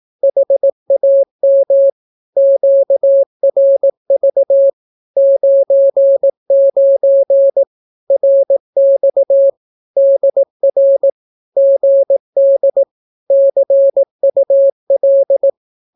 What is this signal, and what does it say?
Morse code, heard as a keyed tone, about 18 wpm, 550 Hz. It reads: HAM QRV 99 RX DR GD CUL
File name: naukapodstaw / skroty / skroty